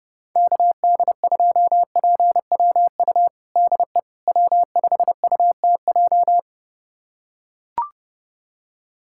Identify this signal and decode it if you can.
Morse code, heard as a keyed tone, about 30 wpm, 700 Hz. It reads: KD2PWU DE W5UTJ
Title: USA Callsign DE Callsign 30wpm